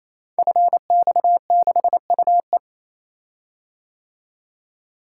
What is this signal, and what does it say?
Morse code, heard as a keyed tone, about 27 wpm, 700 Hz. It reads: FX6UE
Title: Code Groups - 5 Letter:Numbers - 28wpm